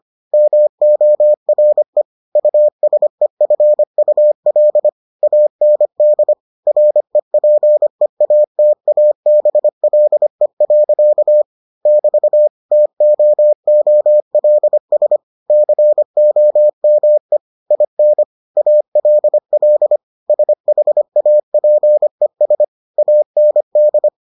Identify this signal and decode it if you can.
Morse code, heard as a keyed tone, about 25 wpm, 600 Hz. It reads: MORE USEFUL AND REPEATABLE. <BT> TOOLS COME IN ALL SHAPES AND